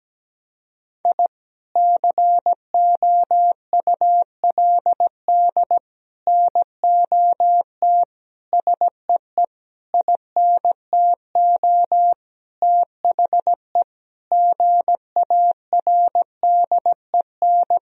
Morse code, heard as a keyed tone, about 17 wpm, 700 Hz. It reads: I COULD NOT SEE INTO THE GARDEN
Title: Sentences from Top 1000 Words - Review Part 4 - 17wpm